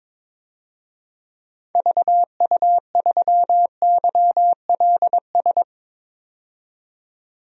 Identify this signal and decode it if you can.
Morse code, heard as a keyed tone, about 22 wpm, 700 Hz. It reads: VU3YLS
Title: DXCC-100 Call Signs - 6 Characters - 22wpm